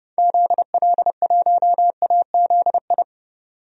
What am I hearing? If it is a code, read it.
Morse code, heard as a keyed tone, about 30 wpm, 700 Hz. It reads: ZL1AZI
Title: DXCC-100 Call Signs - 6 Characters - 30wpm